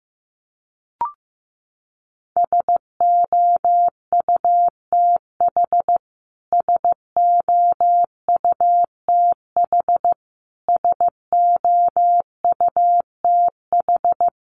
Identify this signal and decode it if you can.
Morse code, heard as a keyed tone, about 15 wpm, 700 Hz. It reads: SOUTH SOUTH SOUTH
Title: Top 500 Words Repeated 15wpm